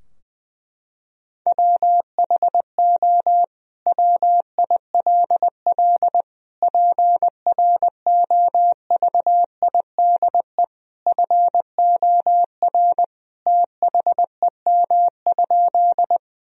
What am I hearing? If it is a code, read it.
Morse code, heard as a keyed tone, about 21 wpm, 700 Hz. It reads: WHO WILL PROVIDE FOR THEM?
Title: Sentences from Top 1000 Words - Review Part 3 - 20wpm